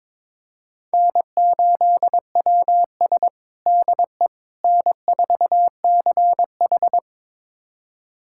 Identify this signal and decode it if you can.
Morse code, heard as a keyed tone, about 22 wpm, 700 Hz. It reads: N8WS DE N4CH
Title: USA Callsign DE Callsign - 4 Characters - 22wpm